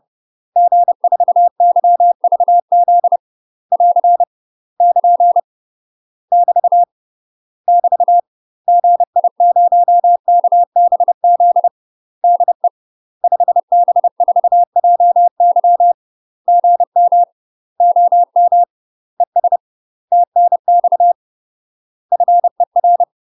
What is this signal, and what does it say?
Morse code, heard as a keyed tone, about 30 wpm, 700 Hz. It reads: G4YVZ <AR> <KN> <BT> <BT> GI0KBZ DE 5B4JY GM OM ES TNX FER
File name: UK MASTER QSOs TRAINING/30WPM